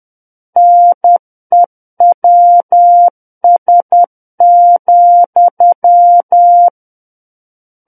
Morse code, heard as a keyed tone, about 10 wpm, 700 Hz. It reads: NEWS,
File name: Book One / 01-Chapter 1 - 10 WPM